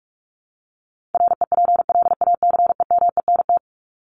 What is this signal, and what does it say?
Morse code, heard as a keyed tone, about 44 wpm, 700 Hz. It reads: REPLACEMENT
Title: Top 1751-2000 Words 45wpm